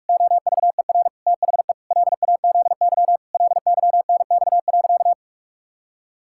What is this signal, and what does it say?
Morse code, heard as a keyed tone, about 46 wpm, 700 Hz. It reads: OVER THE LAZY LYNX.
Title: Pangram Challenge v2 - 45wpm